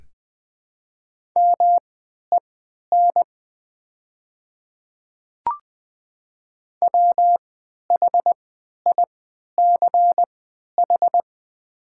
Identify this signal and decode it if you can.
Morse code, heard as a keyed tone, about 20 wpm, spaced as for 11 wpm, 700 Hz. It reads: MEN WHICH
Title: 3x Character Spacing - Top 200 Words - 20wpm